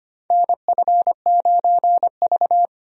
Morse code, heard as a keyed tone, about 25 wpm, 700 Hz. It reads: NF9V
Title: USA Call Signs - 4 Characters - 25wpm